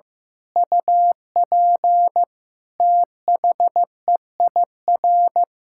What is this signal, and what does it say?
Morse code, heard as a keyed tone, about 15 wpm, 700 Hz. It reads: UP THEIR
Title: Sentences from Top 1600 Words 15wpm